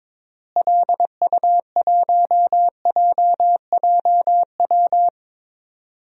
Morse code, heard as a keyed tone, about 22 wpm, 700 Hz. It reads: LU1JJW
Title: DXCC-100 Call Signs - 6 Characters - Repeated 3x - 22wpm